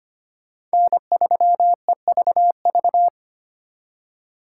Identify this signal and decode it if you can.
Morse code, heard as a keyed tone, about 25 wpm, 700 Hz. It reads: N3EVV
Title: USA Call Signs - 5 Characters - 2x Character Spacing - 25wpm